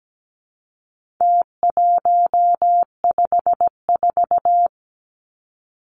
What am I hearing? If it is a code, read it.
Morse code, heard as a keyed tone, about 17 wpm, 700 Hz. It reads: T154
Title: Numbers - Contest Style - 4-Digit 17wpm